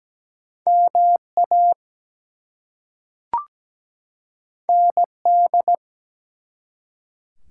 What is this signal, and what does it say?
Morse code, heard as a keyed tone, about 17 wpm, 700 Hz. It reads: MA ND
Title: US States and Canadian Provinces 17wpm